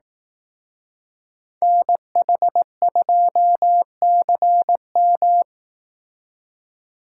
Morse code, heard as a keyed tone, about 18 wpm, 700 Hz. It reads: NH2CM
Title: Speed Racing - USA 2x2 Call Signs 15wpm